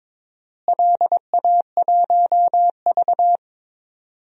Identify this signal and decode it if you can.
Morse code, heard as a keyed tone, about 22 wpm, 700 Hz. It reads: LA1V
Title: DXCC-100 Call Signs - 4 Characters - Repeated 2x - 22wpm